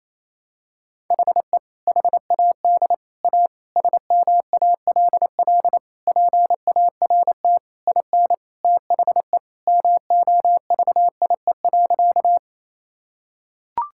Morse code, heard as a keyed tone, about 28 wpm, 700 Hz. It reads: HE HAD A SMALL PART IN THE MOVIE.
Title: Sentences from Top 1000 Words - Review Part 5 - 28wpm